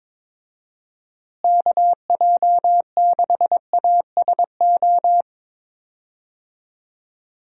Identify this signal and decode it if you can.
Morse code, heard as a keyed tone, about 21 wpm, 700 Hz. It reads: KJ6ASO
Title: USA Call Signs - 6 Characters - Repeated 3x - 22wpm